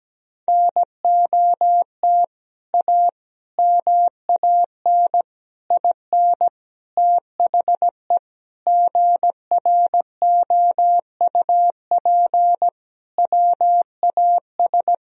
Morse code, heard as a keyed tone, about 17 wpm, 700 Hz. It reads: NOT A MAN IN THE GROUP WAS
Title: Sentences from Top 1000 Words - Review Part 1 - 17wpm